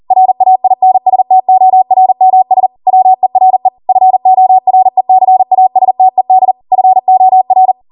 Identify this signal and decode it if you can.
Morse code, heard as a keyed tone, about 40 wpm, 750 Hz. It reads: RAINSTORMS WERE FORECASTED FOR